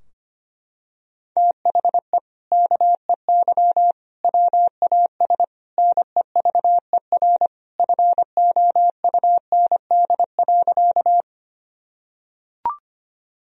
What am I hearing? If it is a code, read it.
Morse code, heard as a keyed tone, about 25 wpm, 700 Hz. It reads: THE KEY WAS NEVER FOUND.
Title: Sentences from Top 1000 Words - Review Part 4 - 25wpm